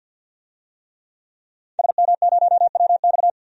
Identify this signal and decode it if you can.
Morse code, heard as a keyed tone, about 47 wpm, 700 Hz. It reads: SM0WX
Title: DXCC-100 Call Signs - 5 Characters - Repeated 2x - 50wpm